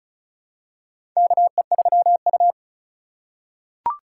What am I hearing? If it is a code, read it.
Morse code, heard as a keyed tone, about 34 wpm, 700 Hz. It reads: KE3U
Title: USA Call Signs - 4 Characters - 2x Character Spacing - 35wpm